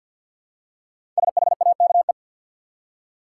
Morse code, heard as a keyed tone, about 50 wpm, 700 Hz. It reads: SHAKE
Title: Five Letter Words - 50wpm